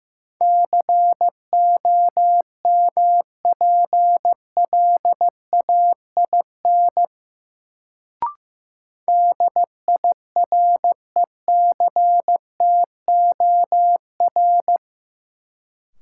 Morse code, heard as a keyed tone, about 15 wpm, 700 Hz. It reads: COMPLAIN DIRECTOR
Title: Sentences from Top 1300 Words - New Word Drill - 15wpm